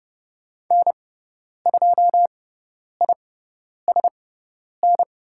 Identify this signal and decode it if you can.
Morse code, heard as a keyed tone, about 29 wpm, spaced as for 10 wpm, 700 Hz. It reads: N2ISN
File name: Lesson 16g - Letter c - Call Signs 30wpm